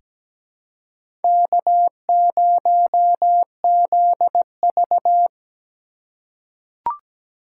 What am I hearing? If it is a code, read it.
Morse code, heard as a keyed tone, about 17 wpm, 700 Hz. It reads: K0ZV
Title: USA Call Signs - 4 Characters - 6x Character Spacing - 17wpm